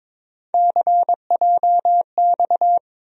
Morse code, heard as a keyed tone, about 22 wpm, 700 Hz. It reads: CJX
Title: USA Call Sign Postfixes - 3 Letters 22wpm